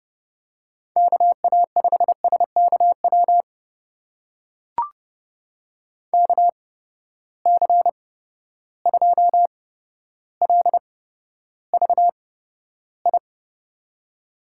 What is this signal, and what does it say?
Morse code, heard as a keyed tone, about 30 wpm, spaced as for 13 wpm, 700 Hz. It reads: KA5SKW KC2LVI
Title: USA Call Signs - 6 Characters - 8x Character Spacing - 30wpm